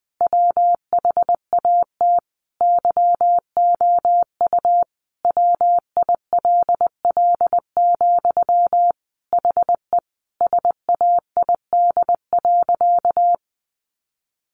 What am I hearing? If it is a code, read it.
Morse code, heard as a keyed tone, about 20 wpm, 700 Hz. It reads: WHAT YOU WILL, HE SAID.
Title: Sentences from Top 200 Words 20wpm